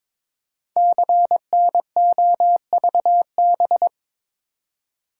Morse code, heard as a keyed tone, about 23 wpm, 700 Hz. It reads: CNOVB
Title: Code Groups - 5 Letters - 22wpm